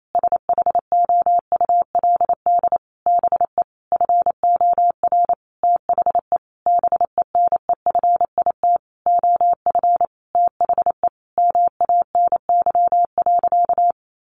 Morse code, heard as a keyed tone, about 28 wpm, 700 Hz. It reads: SHOULD BE FOR THE BENEFIT OF THE MANY.
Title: Sentences from Top 1300 Words 28wpm